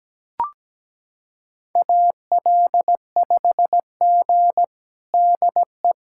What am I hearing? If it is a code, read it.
Morse code, heard as a keyed tone, about 17 wpm, 700 Hz. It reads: AL5G DE
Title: USA Callsign DE Callsign - 4 Characters - 17wpm